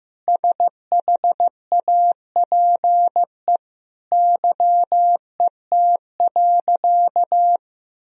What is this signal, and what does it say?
Morse code, heard as a keyed tone, about 15 wpm, 700 Hz. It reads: SHAPE YET.
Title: Sentences from Top 1000 Words - Review Part 3 - 15wpm